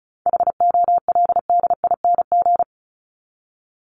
Morse code, heard as a keyed tone, about 34 wpm, 700 Hz. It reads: HOLDING
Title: ING Suffix - Top 100 Words with Base Words 35wpm